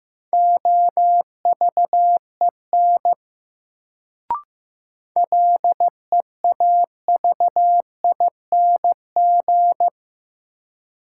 Morse code, heard as a keyed tone, about 15 wpm, 700 Hz. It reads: OVEN LEAVING
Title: Sentences from Top 1000 Words - Review Part 4 - New Word Drill - 15wpm